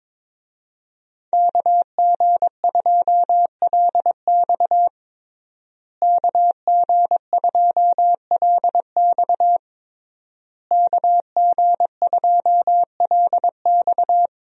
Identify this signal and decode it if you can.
Morse code, heard as a keyed tone, about 21 wpm, 700 Hz. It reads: KG2LX KG2LX KG2LX
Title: USA Call Signs - 5 Characters - Repeated 3x - 22wpm